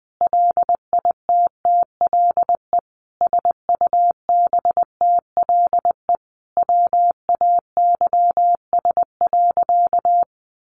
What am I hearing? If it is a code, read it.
Morse code, heard as a keyed tone, about 20 wpm, 700 Hz. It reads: LITTLE SUBTLE WAYS.
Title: A Sound Of Thunder 20wpm